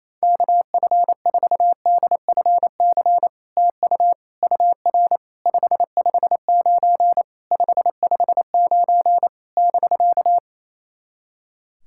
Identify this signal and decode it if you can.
Morse code, heard as a keyed tone, about 28 wpm, 700 Hz. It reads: KF4DFC TU UR 559 559 <BK>
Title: SOTA QSO Practice 28wpm